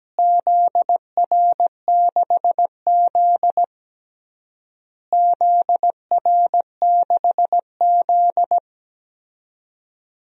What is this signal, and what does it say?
Morse code, heard as a keyed tone, about 17 wpm, 700 Hz. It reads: ZR6Z ZR6Z
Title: DXCC-100 Call Signs - 4 Characters - Repeated 3x - 17wpm